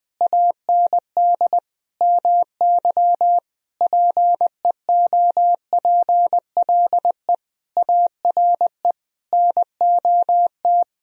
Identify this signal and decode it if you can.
Morse code, heard as a keyed tone, about 20 wpm, 700 Hz. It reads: AND MY PEOPLE ARE NOT
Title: Sentences from Top 1000 Words - Review Part 3 - 20wpm